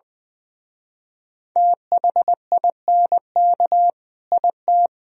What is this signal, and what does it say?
Morse code, heard as a keyed tone, about 24 wpm, 700 Hz. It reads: THINK IT
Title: Speed-Racing - Sets of 2 Words 20wpm